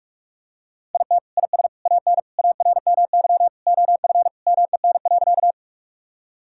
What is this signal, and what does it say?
Morse code, heard as a keyed tone, about 43 wpm, 700 Hz. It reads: IT IS AN ARMY OF MEN.
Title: Sentences from Top 1000 Words - Review Part 5 - 45wpm